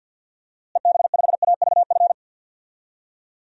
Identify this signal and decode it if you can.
Morse code, heard as a keyed tone, about 51 wpm, 700 Hz. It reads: EB5AVF
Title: DXCC-100 Call Signs - 6 Characters - Repeated 2x - 50wpm